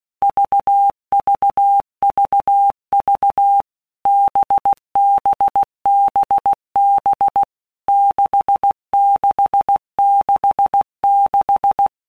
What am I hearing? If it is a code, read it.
Morse code, heard as a keyed tone, about 16 wpm, 800 Hz. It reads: VVVV BBBB 6666
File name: Basic Letter Groups By WPM's / Groups@15 / VB46@15